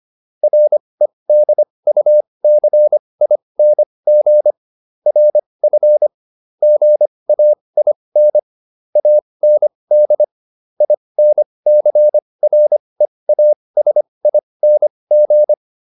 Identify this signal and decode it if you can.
Morse code, heard as a keyed tone, about 25 wpm, spaced as for 20 wpm, 600 Hz. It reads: REDUCING RF GAIN AND INCREASING